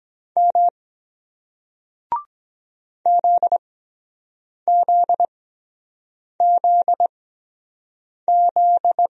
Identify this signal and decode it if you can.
Morse code, heard as a keyed tone, about 23 wpm, spaced as for 7 wpm, 700 Hz. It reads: M ZZZZ